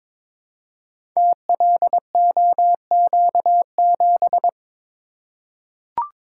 Morse code, heard as a keyed tone, about 22 wpm, 690 Hz. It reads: TLOQ7
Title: Code Groups - 5 Letter:Numbers - 22wpm